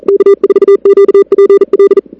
Morse code, heard as a keyed tone, about 35 wpm, 410 Hz. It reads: K4QPL
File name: CWSprint / Feb2006 / 2006-feb-cw-sprint-k5tr op-n5rz -01:00